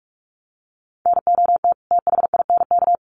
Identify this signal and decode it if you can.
Morse code, heard as a keyed tone, about 44 wpm, 700 Hz. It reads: NOT THINK
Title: Sets of 2 Words 45wpm